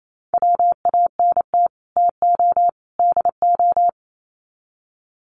Morse code, heard as a keyed tone, about 28 wpm, 700 Hz. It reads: WANT TO DO
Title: Sets of 3 Words 28wpm